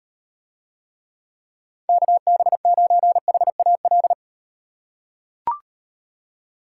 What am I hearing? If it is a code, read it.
Morse code, heard as a keyed tone, about 38 wpm, 700 Hz. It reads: KB9HAL